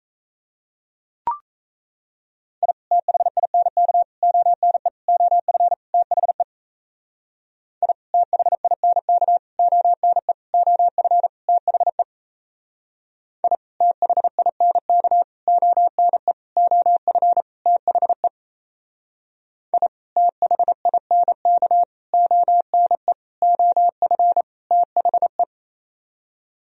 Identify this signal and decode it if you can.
Morse code, heard as a keyed tone, about 36 wpm, 700 Hz. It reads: I THINK ONE OF THE I THINK ONE OF THE I THINK ONE OF THE I THINK ONE OF THE